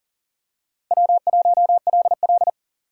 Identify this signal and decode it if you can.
Morse code, heard as a keyed tone, about 40 wpm, 700 Hz. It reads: W1LL
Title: USA Call Signs - 4 Characters - 3x Character Spacing - 40wpm